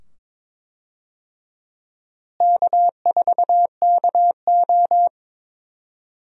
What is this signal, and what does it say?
Morse code, heard as a keyed tone, about 22 wpm, 700 Hz. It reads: K4KO
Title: USA Call Signs - 4 Characters - Repeated 2x - 22wpm